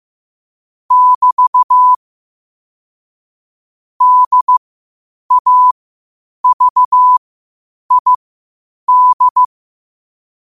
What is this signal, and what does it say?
Morse code, heard as a keyed tone, about 15 wpm, spaced as for 8 wpm, 1000 Hz. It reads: <BT> DAVID
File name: qotdfiles / 8wpm